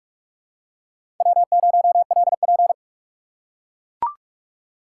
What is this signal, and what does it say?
Morse code, heard as a keyed tone, about 44 wpm, 700 Hz. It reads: W0LP
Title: USA Call Signs - 4 Characters - 6x Character Spacing - 45wpm